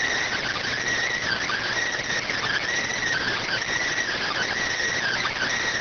Emergency Alert System (SAME) tones (header bursts), heard as harsh buzzing bursts. It is heard from 1.2 to 5.4 s.